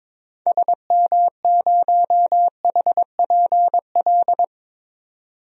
Morse code, heard as a keyed tone, about 22 wpm, 700 Hz. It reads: SM0HPL
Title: DXCC-100 Call Signs - 6 Characters - 22wpm